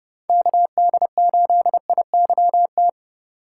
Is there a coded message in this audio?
Yes: Morse code from 0.3 to 2.9 s, heard as a keyed tone, about 30 wpm, 700 Hz. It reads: KD8IYT